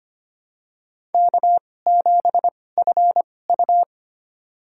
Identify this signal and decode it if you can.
Morse code, heard as a keyed tone, about 25 wpm, spaced as for 18 wpm, 700 Hz. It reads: K7FU
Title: USA Call Signs - 2x Character Spacing - 25wpm